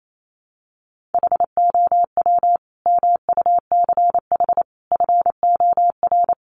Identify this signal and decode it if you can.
Morse code, heard as a keyed tone, about 28 wpm, 700 Hz. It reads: HOW MUCH FOR
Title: Sentences from Top 1500 Words 28wpm